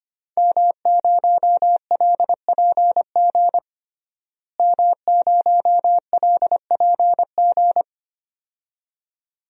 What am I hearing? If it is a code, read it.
Morse code, heard as a keyed tone, about 25 wpm, 700 Hz. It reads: M0LPG M0LPG
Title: DXCC-100 Call Signs - 5 Characters - Repeated 2x - 25wpm